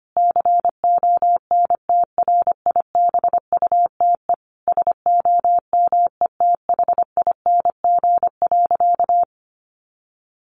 Morse code, heard as a keyed tone, about 25 wpm, 700 Hz. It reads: CONTRIBUTE SOMETHING.
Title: Sentences from Top 1400 Words 25wpm